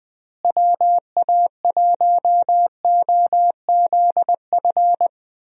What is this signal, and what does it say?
Morse code, heard as a keyed tone, about 20 wpm, 700 Hz. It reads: WA1OZF
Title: USA Call Signs - 6 Characters - Repeated 3x - 20wpm